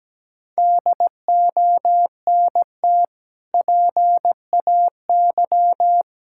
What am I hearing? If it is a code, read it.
Morse code, heard as a keyed tone, about 17 wpm, 700 Hz. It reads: DONT PAY
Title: Sentences from Top 1000 Words - Review Part 6 - 17wpm